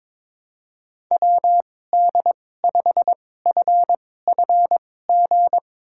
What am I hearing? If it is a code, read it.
Morse code, heard as a keyed tone, about 21 wpm, spaced as for 16 wpm, 700 Hz. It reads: WD5FFG
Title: USA Call Signs - 6 Characters - 2x Character Spacing - 22wpm